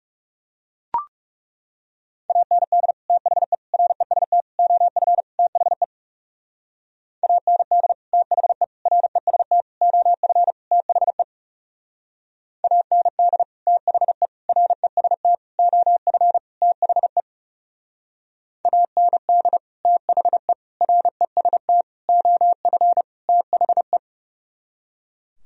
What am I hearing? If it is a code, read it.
Morse code, heard as a keyed tone, about 39 wpm, 700 Hz. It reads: AND THE REST OF THE AND THE REST OF THE AND THE REST OF THE AND THE REST OF THE